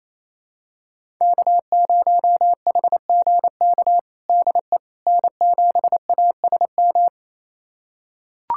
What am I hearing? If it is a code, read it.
Morse code, heard as a keyed tone, about 28 wpm, 700 Hz. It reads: K0HGK DE N7ASM
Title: USA Callsign DE Callsign - 5 Characters - 28wpm